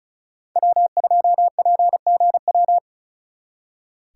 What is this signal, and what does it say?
Morse code, heard as a keyed tone, about 43 wpm, 700 Hz. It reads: W2PGW
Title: Speed Racing - USA 1x3 Call Signs 35wpm